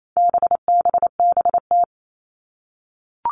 Morse code, heard as a keyed tone, about 28 wpm, 700 Hz. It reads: BBBT
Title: All Cut Numbers - 4-Digit 28wpm